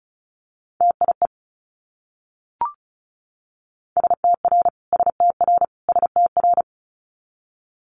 Morse code, heard as a keyed tone, about 36 wpm, 700 Hz. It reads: TIE STR STR STR
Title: Multi-letter Phonemes Repeated 3x 35wpm